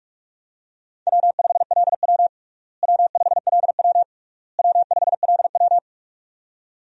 Morse code, heard as a keyed tone, about 42 wpm, 700 Hz. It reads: W5LW W5LW W5LW
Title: USA Call Signs - 4 Characters - Repeated 3x - 45wpm